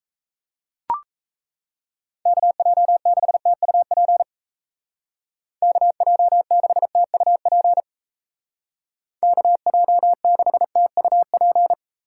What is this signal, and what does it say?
Morse code, heard as a keyed tone, about 37 wpm, 700 Hz. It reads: KJ6TUP KJ6TUP KJ6TUP